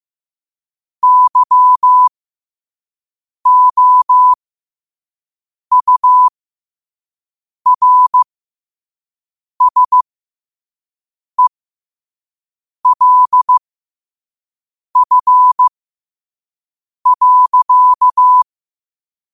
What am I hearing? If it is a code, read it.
Morse code, heard as a keyed tone, about 15 wpm, spaced as for 5 wpm, 1000 Hz. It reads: YOURSELF.
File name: qotdfiles / 5wpm